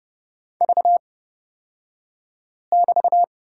Morse code, heard as a keyed tone, about 31 wpm, spaced as for 5 wpm, 700 Hz. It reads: V<BT>
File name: Lesson 36d - BT - Two Characters 30wpm